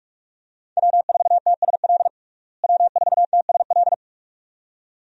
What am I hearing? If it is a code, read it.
Morse code, heard as a keyed tone, about 43 wpm, 700 Hz. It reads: W4TSL W4TSL
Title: USA Call Signs - 5 Characters - Repeated 2x - 45wpm